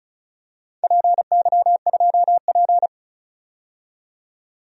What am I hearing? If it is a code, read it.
Morse code, heard as a keyed tone, about 36 wpm, 700 Hz. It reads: PY2P
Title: DXCC-100 Call Signs - 4 Characters - 35wpm